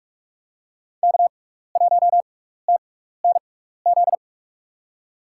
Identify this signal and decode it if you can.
Morse code, heard as a keyed tone, about 46 wpm, spaced as for 15 wpm, 700 Hz. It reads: K1TNZ
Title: USA Call Signs - 6x Character Spacing - 45wpm